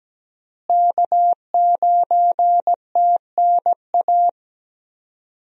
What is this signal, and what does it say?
Morse code, heard as a keyed tone, about 17 wpm, 700 Hz. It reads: K9TNA
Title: USA Call Signs - 5 Characters - Repeated 3x - 17wpm